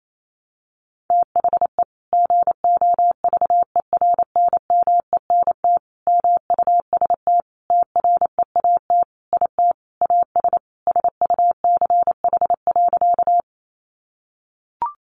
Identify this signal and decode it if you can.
Morse code, heard as a keyed tone, about 28 wpm, 700 Hz. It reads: THE GOVERNMENT MUST TREAT IT AS SUCH.
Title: Sentences from Top 1300 Words 28wpm